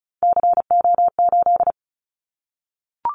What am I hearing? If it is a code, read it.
Morse code, heard as a keyed tone, about 36 wpm, 700 Hz. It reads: CO8
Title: Callsign Prefixes for Top 100 DXCC - 35wpm